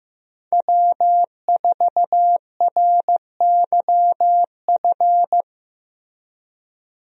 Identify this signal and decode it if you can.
Morse code, heard as a keyed tone, about 15 wpm, 700 Hz. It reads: W4RYF
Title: USA Call Signs - 5 Characters - Repeated 2x - 15wpm